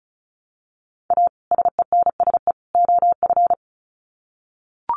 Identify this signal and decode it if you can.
Morse code, heard as a keyed tone, about 34 wpm, 700 Hz. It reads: A SENSE OF
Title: Sets of 3 Words 35wpm